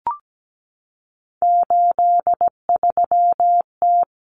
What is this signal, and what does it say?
Morse code, heard as a keyed tone, about 17 wpm, 700 Hz. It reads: 83T
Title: Common Cut Numbers - 3-Digit 17wpm